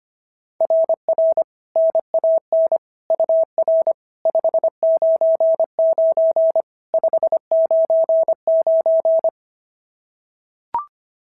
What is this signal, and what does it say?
Morse code, heard as a keyed tone, about 25 wpm, 690 Hz. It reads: RR NAN UR 599 599
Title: SOTA QSO Practice 25wpm